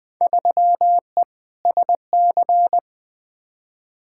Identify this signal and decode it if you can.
Morse code, heard as a keyed tone, about 20 wpm, 700 Hz. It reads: 3E SC
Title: Field Day Exchanges Encore - 20wpm